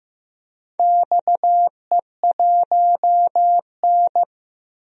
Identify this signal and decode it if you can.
Morse code, heard as a keyed tone, about 15 wpm, 700 Hz. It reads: XE1N
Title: DXCC-100 Call Signs - 4 Characters - Repeated 2x - 15wpm